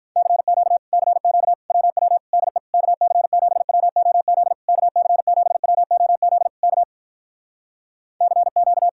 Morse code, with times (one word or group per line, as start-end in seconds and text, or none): 0.2-0.8 CQ
0.9-1.5 CQ
1.7-2.2 WW
2.3-2.6 DE
2.7-4.5 KC7WOZ
4.7-6.5 KC7WOZ
6.6-6.8 K
8.2-8.9 CQ